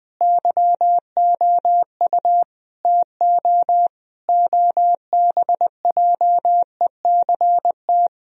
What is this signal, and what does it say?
Morse code, heard as a keyed tone, about 20 wpm, 700 Hz. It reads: YOU TO OBJECT
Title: Sentences from Top 1000 Words - Review Part 3 - 20wpm